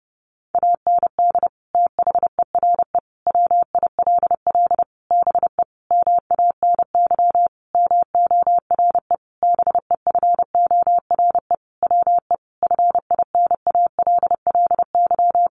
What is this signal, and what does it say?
Morse code, heard as a keyed tone, about 30 wpm, 700 Hz. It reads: AND THERE WILL BE MANY MORE BEFORE WE FINALLY
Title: Sentences from Top 1500 Words 30wpm